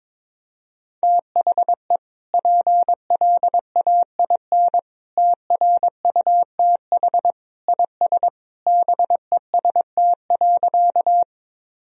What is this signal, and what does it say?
Morse code, heard as a keyed tone, about 22 wpm, 700 Hz. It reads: THE PLAIN TRUTH IS BEST.
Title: Sentences from Top 1000 Words - Review Part 2 - 22wpm